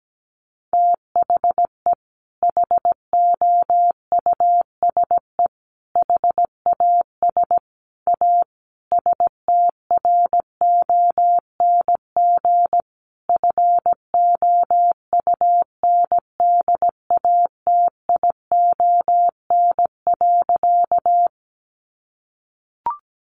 Morse code, heard as a keyed tone, about 17 wpm, 700 Hz. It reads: THE HOUSE HAS A STRONG FOUNDATION.
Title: Sentences from Top 1200 Words 17wpm